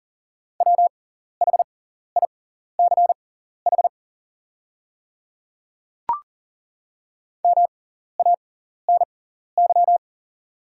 Morse code, heard as a keyed tone, about 38 wpm, spaced as for 14 wpm, 690 Hz. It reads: WHICH MANY
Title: 6x Character Spacing - Top 200 Words - 40wpm